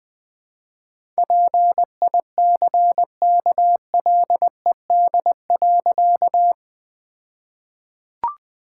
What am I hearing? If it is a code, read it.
Morse code, heard as a keyed tone, about 20 wpm, 700 Hz. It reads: PICKLED.
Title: All-Age Appropriate Jokes - 20wpm